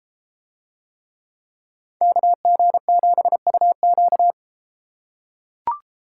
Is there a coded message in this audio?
Yes: Morse code from 2.0 to 4.3 s, heard as a keyed tone, about 31 wpm, 700 Hz. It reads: KG7UQ